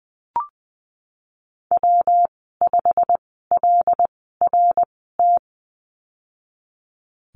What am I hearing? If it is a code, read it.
Morse code, heard as a keyed tone, about 20 wpm, spaced as for 14 wpm, 700 Hz. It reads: W5LRT
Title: 122 Lesson 20g - Number 3 - Call Signs 20wpm